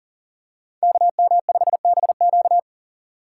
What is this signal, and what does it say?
Morse code, heard as a keyed tone, about 40 wpm, 700 Hz. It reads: KM5BQ
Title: USA Call Signs - 5 Characters - 2x Character Spacing - 40wpm